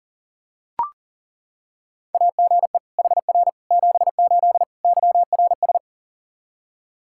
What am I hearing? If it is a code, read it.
Morse code, heard as a keyed tone, about 40 wpm, 700 Hz. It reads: AGE HR 78 YRS
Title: Practice QSO - Exchange 3 - More Info - 40wpm